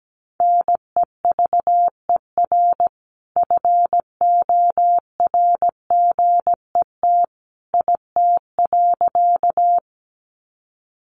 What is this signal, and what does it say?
Morse code, heard as a keyed tone, about 17 wpm, 700 Hz. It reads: NEVER FORGET IT.
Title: Sentences from Top 1200 Words 17wpm